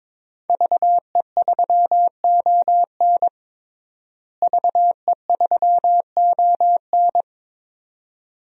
Morse code, heard as a keyed tone, about 21 wpm, 700 Hz. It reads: VE3ON VE3ON
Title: DXCC-100 Call Signs - 5 Characters - Repeated 2x - 22wpm